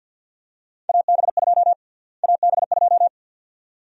Morse code, heard as a keyed tone, about 45 wpm, 700 Hz. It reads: AB2 AB2
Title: USA Call Sign Prefixes - Repeated 2x - 50wpm